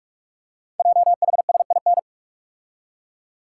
Morse code, heard as a keyed tone, about 44 wpm, 700 Hz. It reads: JHSIN
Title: Code Groups - 5 Letters - 45wpm